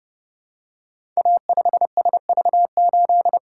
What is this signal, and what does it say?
Morse code, heard as a keyed tone, about 30 wpm, 700 Hz. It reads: A5SV8
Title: Code Groups - 5 Letter:Numbers - 30wpm